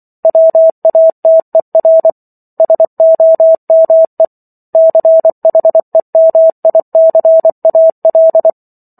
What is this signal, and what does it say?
Morse code, heard as a keyed tone, about 24 wpm, 650 Hz. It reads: WATER SOME CHEMICAL